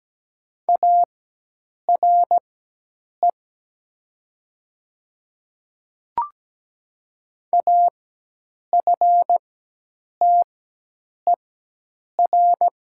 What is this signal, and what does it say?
Morse code, heard as a keyed tone, about 17 wpm, spaced as for 8 wpm, 700 Hz. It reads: ARE AFTER
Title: 4x Character Spacing - Top 200 Words - 17wpm